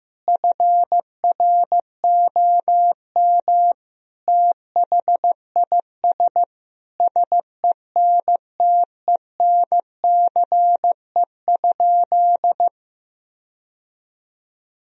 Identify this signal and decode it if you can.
Morse code, heard as a keyed tone, about 15 wpm, 700 Hz. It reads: FROM THIS SENTENCE?
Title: Sentences from Top 1000 Words - Review Part 1 - 15wpm